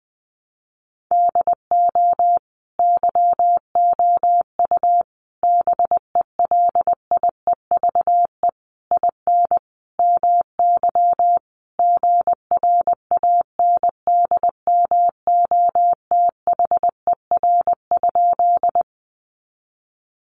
Morse code, heard as a keyed tone, about 20 wpm, 700 Hz. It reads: DO YOU BELIEVE IN MY GRANDMOTHER?
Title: Sentences from Top 1200 Words 20wpm